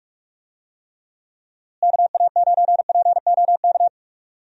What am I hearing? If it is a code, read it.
Morse code, heard as a keyed tone, about 44 wpm, 700 Hz. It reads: KA9POK
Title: USA Call Signs - 6 Characters - Repeated 3x - 45wpm